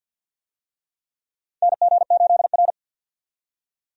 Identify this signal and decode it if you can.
Morse code, heard as a keyed tone, about 51 wpm, 700 Hz. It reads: NG8R
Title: USA Call Signs - 4 Characters - 50wpm